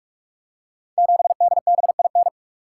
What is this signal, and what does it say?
Morse code, heard as a keyed tone, about 44 wpm, 700 Hz. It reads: 7DBIN
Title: Code Groups - 5 Letter:Numbers - 45wpm